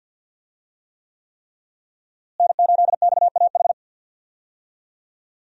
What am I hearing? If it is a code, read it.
Morse code, heard as a keyed tone, about 46 wpm, 690 Hz. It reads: N8XAH
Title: USA Call Signs - 5 Characters - Repeated 3x - 50wpm